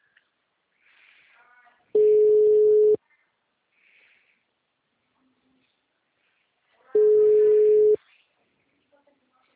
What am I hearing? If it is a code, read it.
Ringback tone (Europe, 425 Hz).
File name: out- -2007